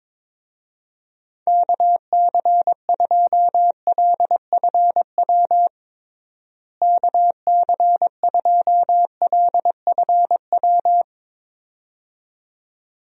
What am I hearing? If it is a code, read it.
Morse code, heard as a keyed tone, about 22 wpm, 700 Hz. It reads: KC2LFW KC2LFW
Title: USA Call Signs - 6 Characters - Repeated 2x - 22wpm